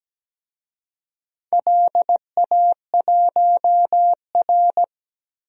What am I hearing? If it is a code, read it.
Morse code, heard as a keyed tone, about 17 wpm, 700 Hz. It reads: LA1R
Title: DXCC-100 Call Signs - 4 Characters - 17wpm